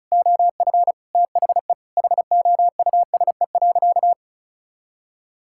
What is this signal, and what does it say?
Morse code, heard as a keyed tone, about 36 wpm, 700 Hz. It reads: OF THE HOUSE.
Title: Sentences from Top 1000 Words - Review Part 5 - 35wpm